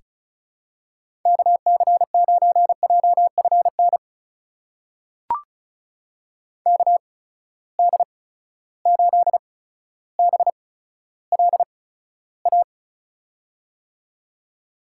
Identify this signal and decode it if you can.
Morse code, heard as a keyed tone, about 36 wpm, spaced as for 15 wpm, 700 Hz. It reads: KC9JFN KD8BLA
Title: USA Call Signs - 6 Characters - 8x Character Spacing - 35wpm